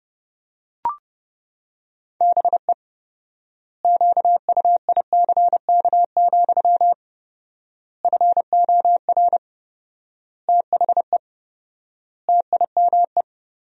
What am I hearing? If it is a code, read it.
Morse code, heard as a keyed tone, about 30 wpm, 700 Hz. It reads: BE QUICK, FOR THE TIME
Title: 4x Word-Spacing - Sentences from Top 500 Words - 30wpm